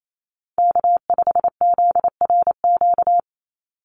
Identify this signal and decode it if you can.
Morse code, heard as a keyed tone, about 28 wpm, 700 Hz. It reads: K5ZRQ
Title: USA Call Signs 28wpm